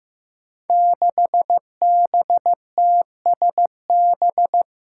Morse code, heard as a keyed tone, about 15 wpm, 700 Hz. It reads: 6BTSB
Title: Five Challenging Letter-Number Combinations 15wpm